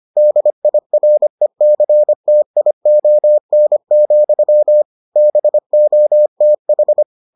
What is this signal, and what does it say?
Morse code, heard as a keyed tone, about 25 wpm, 600 Hz. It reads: DIRECTION, BOTH